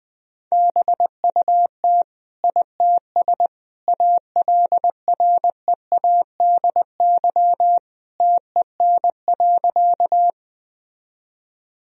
Morse code, heard as a keyed tone, about 20 wpm, 700 Hz. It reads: BUT ITS ALREADY TEN.
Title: Sentences from Top 1000 Words - Review Part 2 - 20wpm